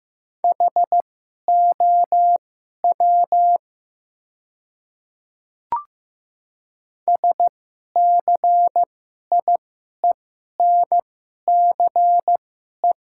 Morse code, heard as a keyed tone, about 15 wpm, spaced as for 11 wpm, 690 Hz. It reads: HOW SCIENCE
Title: 2x Character Spacing - Top 500 Words - 15wpm